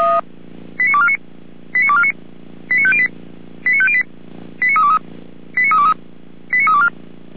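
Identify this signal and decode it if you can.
Five-tone selective call (ZVEI), heard as quick runs of five tones, tones of about 70 ms. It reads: 89249 89249 89508 89508 89323 89323 89324
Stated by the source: ZVEI five-tone call received by SDR, from feuerwehr-orlamuende.org fire brigade audio archive.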